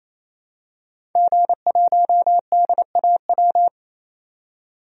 Morse code, heard as a keyed tone, about 28 wpm, 700 Hz. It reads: G1DAW
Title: DXCC-100 Call Signs - 5 Characters - 28wpm